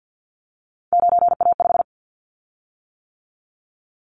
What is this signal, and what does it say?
Morse code, heard as a keyed tone, about 50 wpm, 700 Hz. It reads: 9A5
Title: Callsign Prefixes for Top 25 DXCC - 50wpm